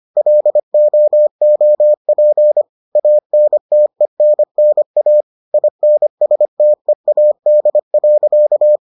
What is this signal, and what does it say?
Morse code, heard as a keyed tone, about 25 wpm, 600 Hz. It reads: LOOP ANTENNA INSTEAD.